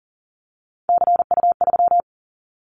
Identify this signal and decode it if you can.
Morse code, heard as a keyed tone, about 39 wpm, 700 Hz. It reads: CU3
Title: Callsign Prefixes for Top 50 DXCC - 40wpm